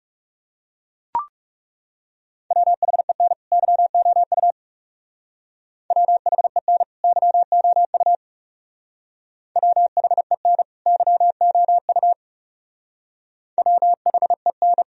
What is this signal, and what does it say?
Morse code, heard as a keyed tone, about 35 wpm, 700 Hz. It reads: WHEN YOU WHEN YOU WHEN YOU WHEN
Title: Speed-Racing - Sets of 2 Words 30wpm